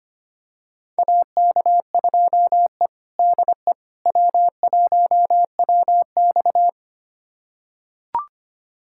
Morse code, heard as a keyed tone, about 25 wpm, 700 Hz. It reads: AK2E DE W1WX
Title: USA Callsign DE Callsign - 4 Characters - 25wpm